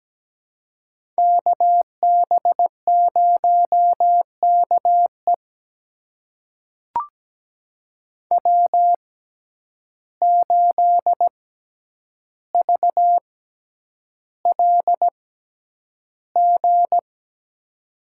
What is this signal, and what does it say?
Morse code, heard as a keyed tone, about 17 wpm, spaced as for 9 wpm, 700 Hz. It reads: KB0KE W8VLG
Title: USA Call Signs - 5 Characters - 6x Character Spacing - 17wpm